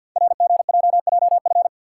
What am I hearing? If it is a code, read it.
Morse code, heard as a keyed tone, about 50 wpm, 700 Hz. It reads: RGJJF
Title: Code Groups - 5 Letters - 50wpm